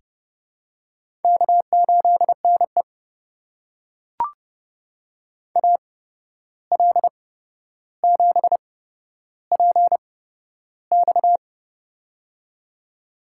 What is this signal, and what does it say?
Morse code, heard as a keyed tone, about 30 wpm, spaced as for 12 wpm, 700 Hz. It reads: K8NE AL7PX
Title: USA Call Signs - 8x Character Spacing - 30wpm